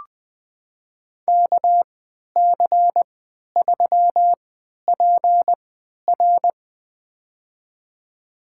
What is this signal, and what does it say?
Morse code, heard as a keyed tone, about 20 wpm, spaced as for 11 wpm, 700 Hz. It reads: KC3PR
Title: USA Call Signs - 3x Character Spacing - 20wpm